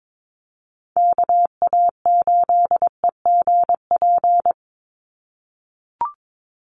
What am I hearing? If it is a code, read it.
Morse code, heard as a keyed tone, about 22 wpm, 700 Hz. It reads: KA8EGP
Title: USA Call Signs 22wpm